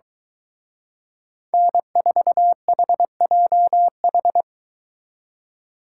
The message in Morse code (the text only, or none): N4HJH